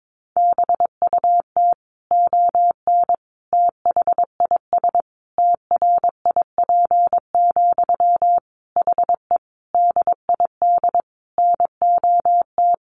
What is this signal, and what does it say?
Morse code, heard as a keyed tone, about 22 wpm, 700 Hz. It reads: BUT ON THIS TRIP, HE DID NOT
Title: Sentences from Top 1400 Words 22wpm